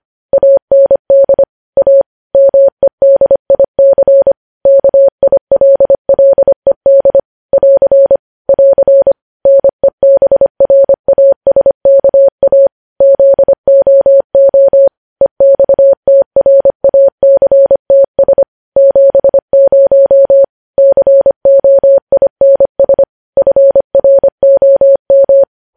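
Morse code, heard as a keyed tone, about 25 wpm, 550 Hz. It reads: AND A MEDIC KILLED <AR> <AR> NEBRASKA ZOO EXTRACTS 70 COINS FROM
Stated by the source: Morse code news headlines sent at 25 wpm.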